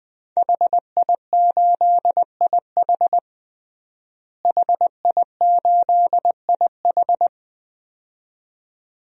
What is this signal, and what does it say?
Morse code, heard as a keyed tone, about 20 wpm, 700 Hz. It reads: HI8IH HI8IH
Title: DXCC-100 Call Signs - 5 Characters - Repeated 2x - 20wpm